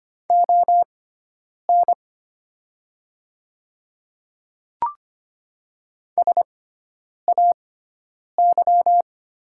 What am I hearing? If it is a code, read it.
Morse code, heard as a keyed tone, about 25 wpm, spaced as for 9 wpm, 700 Hz. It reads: ON SAY
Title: 6x Character Spacing - Top 200 Words - 25wpm